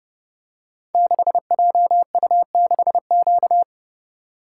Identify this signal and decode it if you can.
Morse code, heard as a keyed tone, about 30 wpm, 700 Hz. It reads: 6JU6Q
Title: Code Groups - 5 Letter:Numbers - 30wpm